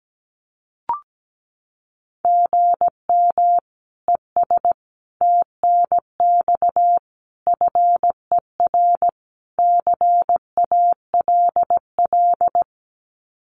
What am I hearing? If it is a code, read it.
Morse code, heard as a keyed tone, about 17 wpm, 700 Hz. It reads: GM ES TNX FER CALL
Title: Common QSO Phrases Encore 17wpm